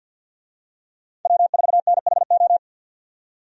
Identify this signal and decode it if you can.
Morse code, heard as a keyed tone, about 50 wpm, 700 Hz. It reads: W4NHO
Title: USA Call Signs - 5 Characters - 50wpm